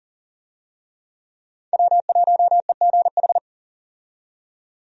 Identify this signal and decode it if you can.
Morse code, heard as a keyed tone, about 40 wpm, 700 Hz. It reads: W1EGH
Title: USA Call Signs - 5 Characters - 40wpm